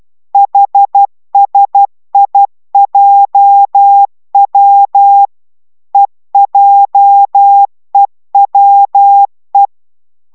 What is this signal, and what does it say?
Morse code, heard as a keyed tone, about 12 wpm, 800 Hz. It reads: HSIJW EJEWE